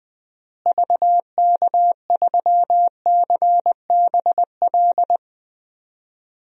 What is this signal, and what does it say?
Morse code, heard as a keyed tone, about 20 wpm, 700 Hz. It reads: VK3CBL
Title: DXCC-100 Call Signs - 6 Characters - Repeated 2x - 20wpm